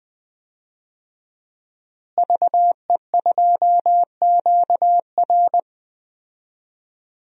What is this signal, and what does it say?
Morse code, heard as a keyed tone, about 21 wpm, 700 Hz. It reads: VE2QR
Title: DXCC-100 Call Signs - 5 Characters - Repeated 3x - 20wpm